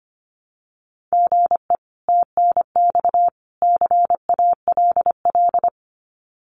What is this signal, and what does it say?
Morse code, heard as a keyed tone, about 25 wpm, 700 Hz. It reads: GE TNX CALL
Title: Common QSO Phrases 25wpm